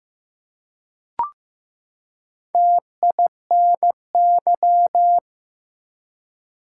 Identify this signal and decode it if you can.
Morse code, heard as a keyed tone, about 15 wpm, 700 Hz. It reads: TINY
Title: Four Letter Words - 15wpm